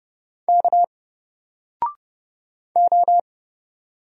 Morse code, heard as a keyed tone, about 29 wpm, 700 Hz. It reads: K O
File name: Single-Letter-Number-30wpm